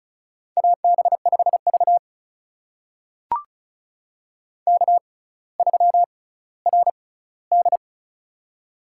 Morse code, heard as a keyed tone, about 36 wpm, spaced as for 18 wpm, 700 Hz. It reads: AB5V K3RD
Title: USA Call Signs - 4 Characters - 6x Character Spacing - 35wpm